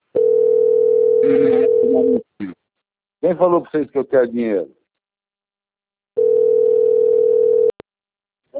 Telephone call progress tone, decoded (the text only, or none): ringback tone